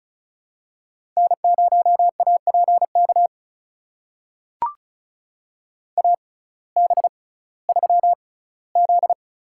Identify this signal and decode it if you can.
Morse code, heard as a keyed tone, about 36 wpm, spaced as for 19 wpm, 700 Hz. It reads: N0APK AB3Z
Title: USA Call Signs - 5 Characters - 6x Character Spacing - 35wpm